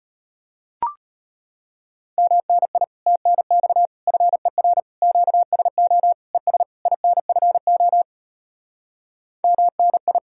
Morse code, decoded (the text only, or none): MNI TNX FER QSO ES INFO MNI